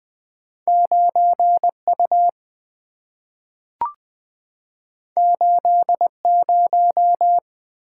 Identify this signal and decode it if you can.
Morse code, heard as a keyed tone, about 20 wpm, 700 Hz. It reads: 9 U 8 0
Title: Two Letter-Number Combinations - 20wpm